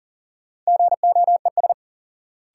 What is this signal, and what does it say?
Morse code, heard as a keyed tone, about 38 wpm, 700 Hz. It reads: GOES
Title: Sentences from Top 1500 Words - New Word Drill Alt - 40wpm